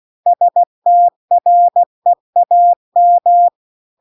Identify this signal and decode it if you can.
Morse code, heard as a keyed tone, about 16 wpm, 700 Hz. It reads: STREAM
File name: CW-The-Big-Twoheaded-River-2-16WPM